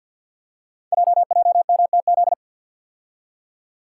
Morse code, heard as a keyed tone, about 48 wpm, 690 Hz. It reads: JJMTZ
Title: Code Groups - 5 Letters - 50wpm